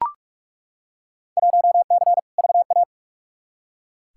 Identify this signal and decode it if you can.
Morse code, heard as a keyed tone, about 43 wpm, 700 Hz. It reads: 1C VA
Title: Field Day Exchanges Encore - 45wpm